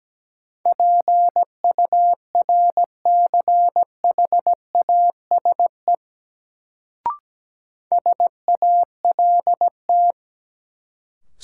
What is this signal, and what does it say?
Morse code, heard as a keyed tone, about 17 wpm, 700 Hz. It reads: PURCHASE SALT
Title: Sentences from Top 800 Words - New Word Drill - 17wpm